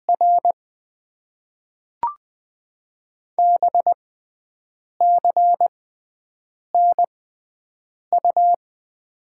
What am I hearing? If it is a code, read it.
Morse code, heard as a keyed tone, about 20 wpm, spaced as for 7 wpm, 700 Hz. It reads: R BCNU
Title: Top 90 QSO Elements - 6x Character Spacing - 20wpm